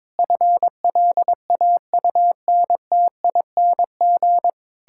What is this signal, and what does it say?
Morse code, heard as a keyed tone, about 22 wpm, 700 Hz. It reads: FLAUNTING
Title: Pangram Challenge v2 - 22wpm